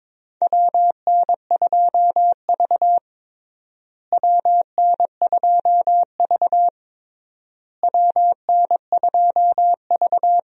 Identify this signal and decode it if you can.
Morse code, heard as a keyed tone, about 21 wpm, 700 Hz. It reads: WN2V WN2V WN2V
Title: USA Call Signs - 4 Characters - Repeated 3x - 22wpm